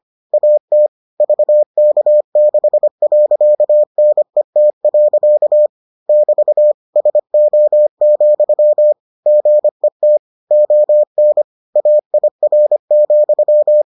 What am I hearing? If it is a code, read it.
Morse code, heard as a keyed tone, about 25 wpm, 600 Hz. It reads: AT VK6.NET. <BT> SO, GET ON AIR,